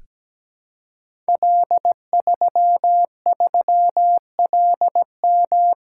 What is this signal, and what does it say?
Morse code, heard as a keyed tone, about 17 wpm, 700 Hz. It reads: L33LM
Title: Code Groups - 5 Letter:Numbers - 17wpm